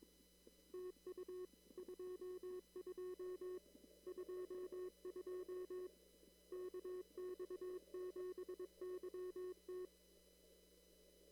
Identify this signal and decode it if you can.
Morse code, heard as a keyed tone, about 22 wpm, 370 Hz. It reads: TU 22 22 KX7YT
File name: CW / kt5j wpx cw-op-k5pi